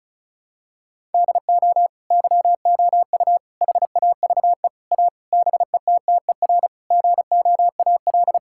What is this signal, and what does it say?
Morse code, heard as a keyed tone, about 36 wpm, 700 Hz. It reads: DO YOU HAVE A BETTER GOAL
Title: Sentences from Top 1000 Words - Review Part 5 - 35wpm